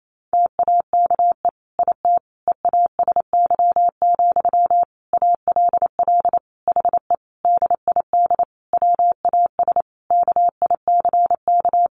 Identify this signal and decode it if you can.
Morse code, heard as a keyed tone, about 28 wpm, 700 Hz. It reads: TAKE IT EASY, ALL HE DID WAS KICK
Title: A Sound Of Thunder 28wpm